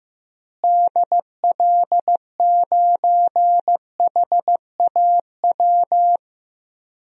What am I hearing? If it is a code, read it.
Morse code, heard as a keyed tone, about 14 wpm, 700 Hz. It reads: DL9HAW
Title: DXCC-100 Call Signs - 6 Characters - Repeated 3x - 15wpm